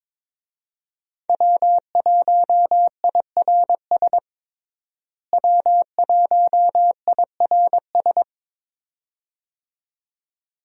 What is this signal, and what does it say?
Morse code, heard as a keyed tone, about 22 wpm, 700 Hz. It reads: W1IRS W1IRS
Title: USA Call Signs - 5 Characters - Repeated 2x - 22wpm